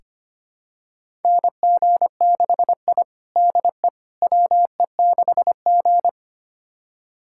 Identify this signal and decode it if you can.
Morse code, heard as a keyed tone, about 25 wpm, 700 Hz. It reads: NG6I DE WE6G
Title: USA Callsign DE Callsign - 4 Characters - 25wpm